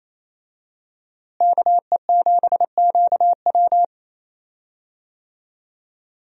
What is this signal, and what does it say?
Morse code, heard as a keyed tone, about 28 wpm, 700 Hz. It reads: KE7QW
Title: USA Call Signs - 5 Characters - Repeated 2x - 28wpm